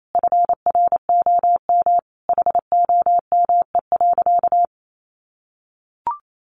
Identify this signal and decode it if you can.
Morse code, heard as a keyed tone, about 28 wpm, 700 Hz. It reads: FROM HOME.
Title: Sentences from Top 1500 Words 28wpm